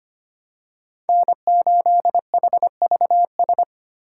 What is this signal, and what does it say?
Morse code, heard as a keyed tone, about 25 wpm, 700 Hz. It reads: N8HVS
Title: USA Call Signs - 5 Characters - 6x Character Spacing - 25wpm